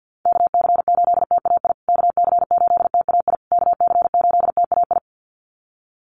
Morse code, heard as a keyed tone, about 50 wpm, 700 Hz. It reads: KC8TAI KC8TAI KC8TAI
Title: USA Call Signs Repeated 3x 50wpm